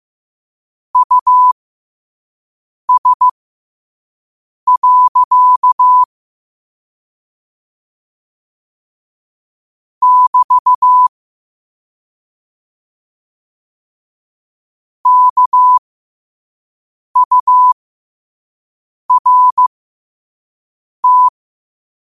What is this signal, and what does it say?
Morse code, heard as a keyed tone, about 15 wpm, spaced as for 5 wpm, 1000 Hz. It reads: US. <BT> KURT
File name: qotdfiles / 5wpm